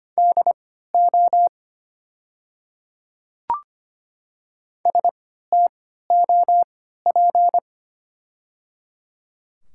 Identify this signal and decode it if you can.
Morse code, heard as a keyed tone, about 25 wpm, spaced as for 14 wpm, 700 Hz. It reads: DO STOP
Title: 3x Character Spacing - Top 300 Words - 25wpm